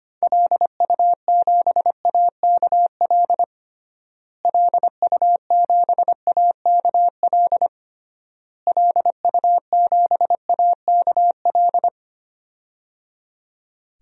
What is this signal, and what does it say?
Morse code, heard as a keyed tone, about 25 wpm, 700 Hz. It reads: LU7AKL LU7AKL LU7AKL
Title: DXCC-100 Call Signs - 6 Characters - Repeated 3x - 25wpm